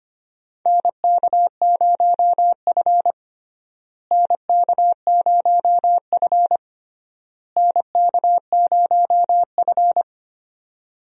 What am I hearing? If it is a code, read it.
Morse code, heard as a keyed tone, about 25 wpm, 700 Hz. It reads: NK0F NK0F NK0F
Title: USA Call Signs - 4 Characters - Repeated 3x - 25wpm